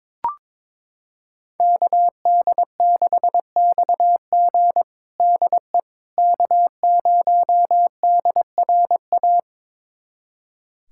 Morse code, heard as a keyed tone, about 22 wpm, 700 Hz. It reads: KD6XG DE K0DRA
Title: USA Callsign DE Callsign - 5 Characters - 22wpm